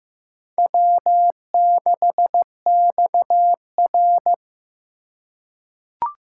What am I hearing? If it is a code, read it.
Morse code, heard as a keyed tone, about 15 wpm, 700 Hz. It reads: W6XR
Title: K1USN SST QSO Practice - 15wpm